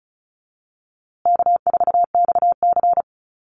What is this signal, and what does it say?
Morse code, heard as a keyed tone, about 34 wpm, 700 Hz. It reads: K4XC
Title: USA Call Signs Repeated 3x 35wpm